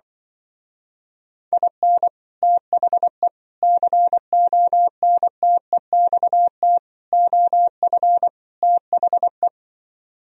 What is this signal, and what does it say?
Morse code, heard as a keyed tone, about 23 wpm, 700 Hz. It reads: IN THE CONTEXT OF THE
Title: Speed-Racing - Sets of 5 Words 20wpm